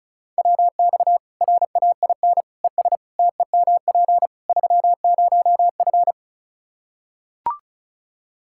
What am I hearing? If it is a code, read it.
Morse code, heard as a keyed tone, about 36 wpm, 700 Hz. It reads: WX RAIN ES TEMP 30F
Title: K1USN SST QSO Practice 35wpm